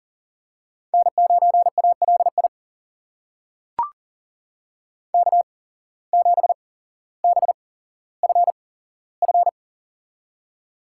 Morse code, heard as a keyed tone, about 40 wpm, spaced as for 17 wpm, 700 Hz. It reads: N9ALI K7BFF
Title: USA Call Signs - 5 Characters - 8x Character Spacing - 40wpm